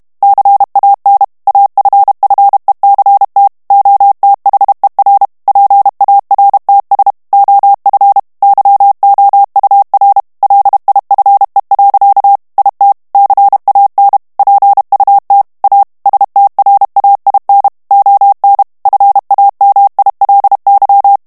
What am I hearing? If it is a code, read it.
Morse code, heard as a keyed tone, about 32 wpm, 800 Hz. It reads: CAN AFFECT OTHER PARTS OF YOUR LIFE. IT CAN PUT A STRAIN ON FAMILY
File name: morse mp3 / ANGLAIS texte clair 1 vitesse 30 mots minute